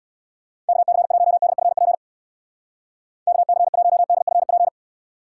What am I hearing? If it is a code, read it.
Morse code, heard as a keyed tone, about 63 wpm, 700 Hz. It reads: KC1DFP KC1DFP